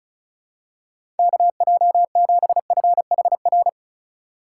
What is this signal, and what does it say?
Morse code, heard as a keyed tone, about 36 wpm, 700 Hz. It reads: KJ7FHR
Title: USA Call Signs - 6 Characters - 4x Character Spacing - 35wpm